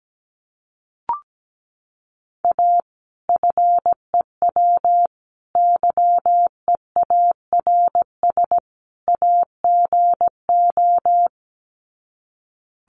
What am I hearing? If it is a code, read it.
Morse code, heard as a keyed tone, about 17 wpm, 700 Hz. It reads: A FEW YEARS AGO
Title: Sets of 4 Words 17wpm